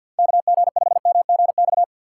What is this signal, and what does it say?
Morse code, heard as a keyed tone, about 52 wpm, 700 Hz. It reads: KG5MGX
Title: USA Call Signs - 3x Character Spacing - 50wpm